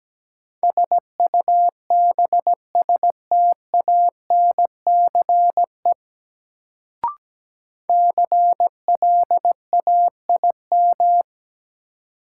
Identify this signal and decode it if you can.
Morse code, heard as a keyed tone, about 17 wpm, 700 Hz. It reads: SUBSTANCE CLAIM
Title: Sentences from Top 1500 Words - New Word Drill Alt - 17wpm